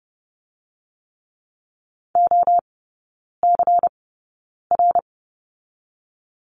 Morse code, heard as a keyed tone, about 30 wpm, 700 Hz. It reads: O C R
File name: Lesson 19e - Letter w - Three Characters 30wpm